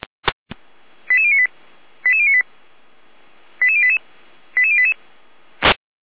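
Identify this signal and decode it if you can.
Five-tone selective call (ZVEI), heard as quick runs of five tones, tones of about 70 ms. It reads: 88098 88098 88088 88088
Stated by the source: ZVEI five-tone selective call 88571 received by SDR.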